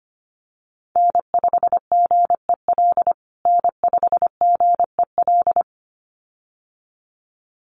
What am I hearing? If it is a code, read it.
Morse code, heard as a keyed tone, about 25 wpm, 700 Hz. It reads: N5GEL N5GEL
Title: USA Call Signs Repeated 2x 25wpm